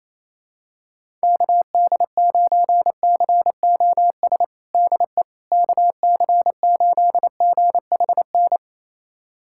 Morse code, heard as a keyed tone, about 28 wpm, 700 Hz. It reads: KD9COS DE KC8GHN
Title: USA Callsign DE Callsign - 6 Characters - 28wpm